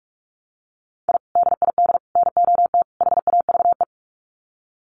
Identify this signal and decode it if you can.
Morse code, heard as a keyed tone, about 44 wpm, 700 Hz. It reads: I DID NOT HAVE
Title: Sets of 4 Words 45wpm